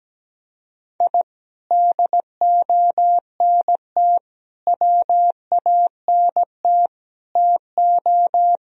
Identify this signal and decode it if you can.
Morse code, heard as a keyed tone, about 17 wpm, 700 Hz. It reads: I DONT WANT TO
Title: Sentences from Top 1000 Words - Review Part 1 - 17wpm